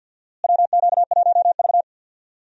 Morse code, heard as a keyed tone, about 52 wpm, 700 Hz. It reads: WQ1V
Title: USA Call Signs - 4 Characters - Repeated 3x - 50wpm